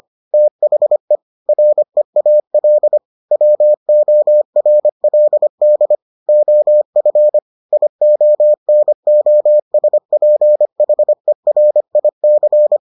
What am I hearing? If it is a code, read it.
Morse code, heard as a keyed tone, about 25 wpm, 600 Hz. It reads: THE REAL WORLD OF IONOSPHERIC